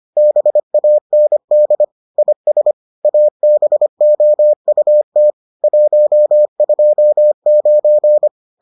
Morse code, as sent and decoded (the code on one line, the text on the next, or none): -... .- -. -.. / .. ... / .- -... --- ..- - / .---- ..--- ----.
BAND IS ABOUT 129